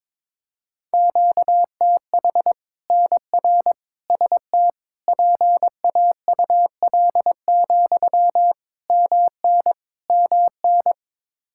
Morse code, heard as a keyed tone, about 22 wpm, 700 Hz. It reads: QTH NR ST PAUL, MN MN
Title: Practice QSO - Exchange 1 - Beginning - 22wpm